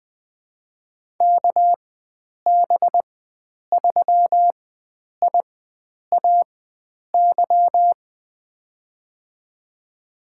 Morse code, heard as a keyed tone, about 20 wpm, spaced as for 9 wpm, 700 Hz. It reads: KB3IAY
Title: USA Call Signs - 6 Characters - 4x Character Spacing - 20wpm